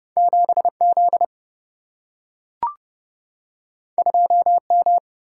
Morse code, heard as a keyed tone, about 30 wpm, 700 Hz. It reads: 7 Z 2 M
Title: Single-Letter Single-Number 30wpm